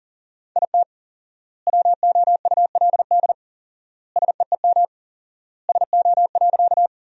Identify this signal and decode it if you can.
Morse code, heard as a keyed tone, about 40 wpm, 700 Hz. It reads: IT WOULD SEEM SO.
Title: 4x Word-Spacing - Sentences from Top 300 Words - 40wpm